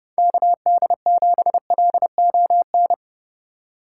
Morse code, heard as a keyed tone, about 32 wpm, 700 Hz. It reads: KD7LON
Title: USA Call Signs - 4x Character Spacing - 30wpm